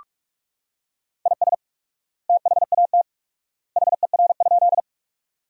Morse code, heard as a keyed tone, about 46 wpm, 700 Hz. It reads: IS THAT HER?
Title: 4x Word-Spacing - Sentences from Top 100 Words - 45wpm